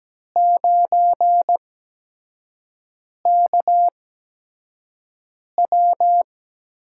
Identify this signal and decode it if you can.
Morse code, heard as a keyed tone, about 17 wpm, spaced as for 5 wpm, 700 Hz. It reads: 9KW
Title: USA Call Signs - 8x Character Spacing - 17wpm